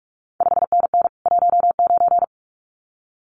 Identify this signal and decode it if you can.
Morse code, heard as a keyed tone, about 46 wpm, 700 Hz. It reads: 5NN 19
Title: CQ WW DX Contest Exchange Encore - 45wpm